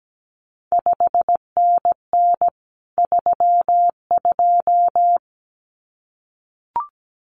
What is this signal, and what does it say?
Morse code, heard as a keyed tone, about 17 wpm, 700 Hz. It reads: 5NN 32
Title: CQ WW DX Contest Exchange Encore - 17wpm